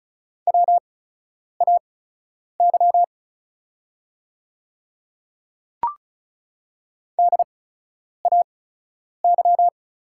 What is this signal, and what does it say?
Morse code, heard as a keyed tone, about 36 wpm, spaced as for 10 wpm, 700 Hz. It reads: WAY DAY
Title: 8x Character Spacing - Top 100 Words - 35wpm